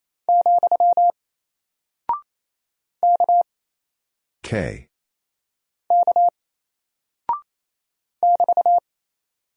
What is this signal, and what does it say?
Morse code, heard as a keyed tone, about 27 wpm, 700 Hz. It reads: , K K <BT>
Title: Single Character (Letter:Number:Punctuation:Prosign) Alt 28wpm